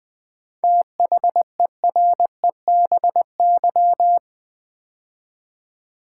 Morse code, heard as a keyed tone, about 20 wpm, 700 Hz. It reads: THEREBY
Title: 7 Letter Words - First 3 Letters Given - 20wpm